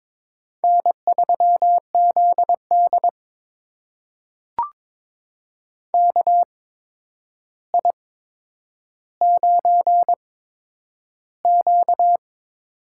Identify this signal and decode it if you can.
Morse code, heard as a keyed tone, about 22 wpm, spaced as for 9 wpm, 700 Hz. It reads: N3ZD KI9Q
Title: USA Call Signs - 4 Characters - 8x Character Spacing - 22wpm